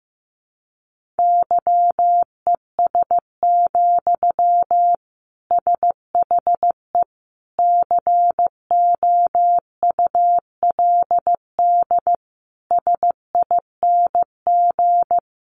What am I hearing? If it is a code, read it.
Morse code, heard as a keyed tone, about 15 wpm, 700 Hz. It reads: YES, SHE COULD SING
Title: Sentences from Top 500 Words 15wpm